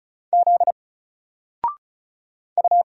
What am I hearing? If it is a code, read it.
Morse code, heard as a keyed tone, about 34 wpm, 700 Hz. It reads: Z U
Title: Single Letter-Number 35wpm